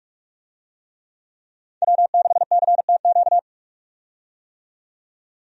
Morse code, heard as a keyed tone, about 46 wpm, 700 Hz. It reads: W6CTQ
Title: USA Call Signs - 5 Characters - 45wpm